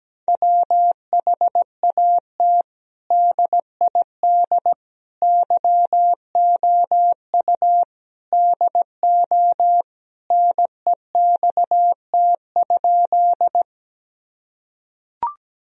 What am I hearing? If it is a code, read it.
Morse code, heard as a keyed tone, about 17 wpm, 700 Hz. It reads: WHAT DID YOU DO NEXT?
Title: Sentences from Top 1000 Words - Review Part 1 - 17wpm